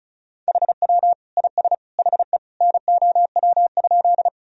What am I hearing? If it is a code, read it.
Morse code, heard as a keyed tone, about 36 wpm, 700 Hz. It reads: HW IS HE NOW?
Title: Sentences from Top 100 Words and Abbreviations 35wpm